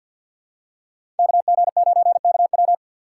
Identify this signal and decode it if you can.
Morse code, heard as a keyed tone, about 51 wpm, 690 Hz. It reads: KG9KW
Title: USA Call Signs - 5 Characters - 6x Character Spacing - 50wpm